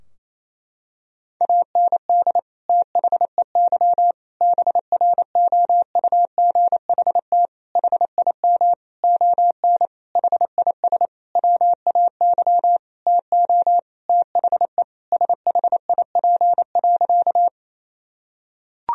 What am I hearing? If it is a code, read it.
Morse code, heard as a keyed tone, about 28 wpm, 700 Hz. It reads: AND THEY BROUGHT HIM ON HIS WAY TO THE SHIP.
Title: Sentences from Top 1000 Words - Review Part 2 - 28wpm